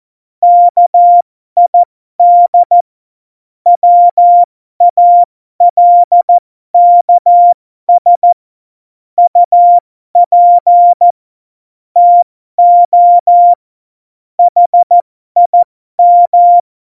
Morse code, heard as a keyed tone, about 14 wpm, spaced as for 12 wpm, 700 Hz. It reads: KID WALKS UP TO HIM
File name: Morsekurs / Witze